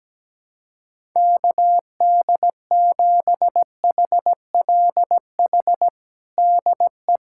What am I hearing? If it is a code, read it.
Morse code, heard as a keyed tone, about 17 wpm, 700 Hz. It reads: KD7HLH DE
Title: USA Callsign DE Callsign - 6 Characters - 17wpm